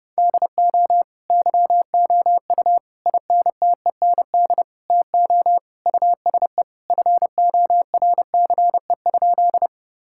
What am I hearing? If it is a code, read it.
Morse code, heard as a keyed tone, about 30 wpm, 700 Hz. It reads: DO YOU INTEND TO USE FORCE?
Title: Sentences from Top 1600 Words 30wpm